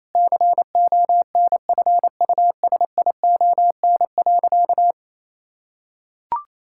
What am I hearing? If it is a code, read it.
Morse code, heard as a keyed tone, about 28 wpm, 700 Hz. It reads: CONFUSION.
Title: Sentences from Top 1600 Words 28wpm